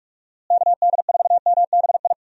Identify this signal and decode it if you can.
Morse code, heard as a keyed tone, about 42 wpm, 700 Hz. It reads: KD4MBI
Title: USA Call Signs - 6 Characters - Repeated 2x - 45wpm